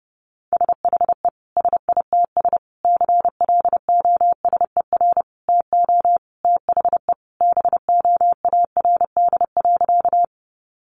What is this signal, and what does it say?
Morse code, heard as a keyed tone, about 30 wpm, 700 Hz. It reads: SHE SITS CLOSER TO THE BOARD.
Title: Additional Easy Phrases 30wpm